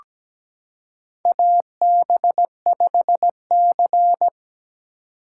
Morse code, heard as a keyed tone, about 17 wpm, 700 Hz. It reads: AB5C
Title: K1USN SST QSO Practice - 17wpm